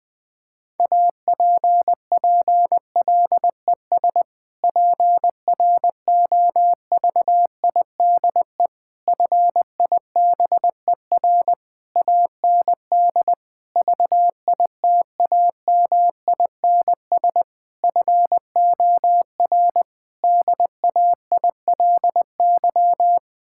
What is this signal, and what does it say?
Morse code, heard as a keyed tone, about 21 wpm, 700 Hz. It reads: APPLES PROVIDE FIBER AND VITAMINS FOR DAILY
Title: Difficult Sentences with Hints - 20wpm